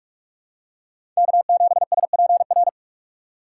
Morse code, heard as a keyed tone, about 45 wpm, 700 Hz. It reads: K7SPR
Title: USA Call Signs - 5 Characters - 3x Character Spacing - 45wpm